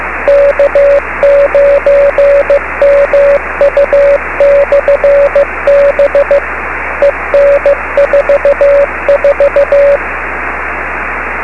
Morse code, heard as a keyed tone, about 15 wpm, 570 Hz, with radio noise. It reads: K9MU/B EN44
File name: beacons / k9mu